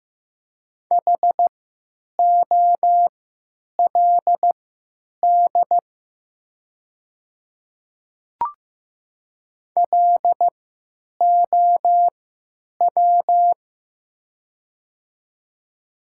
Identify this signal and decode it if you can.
Morse code, heard as a keyed tone, about 15 wpm, spaced as for 9 wpm, 700 Hz. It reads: HOLD LOW
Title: 3x Character Spacing - Top 500 Words - 15wpm